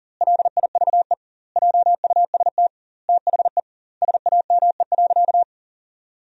Morse code, heard as a keyed tone, about 40 wpm, 700 Hz. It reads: LIVE JUST THE SAME.
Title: 2x Word-Spacing - Sentences from Top 300 Words - 40wpm